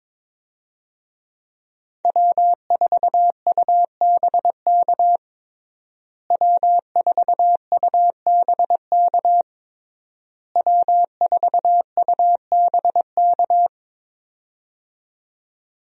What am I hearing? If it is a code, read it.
Morse code, heard as a keyed tone, about 22 wpm, 700 Hz. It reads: W4UBK W4UBK W4UBK
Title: USA Call Signs - 5 Characters - Repeated 3x - 22wpm